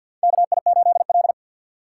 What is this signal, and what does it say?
Morse code, heard as a keyed tone, about 52 wpm, 700 Hz. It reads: K I 8 L
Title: USA Call Signs - 6x Character Spacing - 50wpm